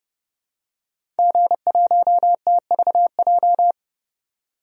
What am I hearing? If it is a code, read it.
Morse code, heard as a keyed tone, about 30 wpm, 700 Hz. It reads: G1TVJ
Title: DXCC-100 Call Signs - 5 Characters - 30wpm